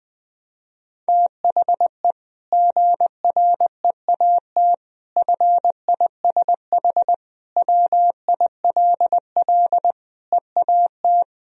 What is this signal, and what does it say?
Morse code, heard as a keyed tone, about 20 wpm, 700 Hz. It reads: THE GREAT FISH WILL EAT
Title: Sentences from Top 1000 Words - Review Part 1 - 20wpm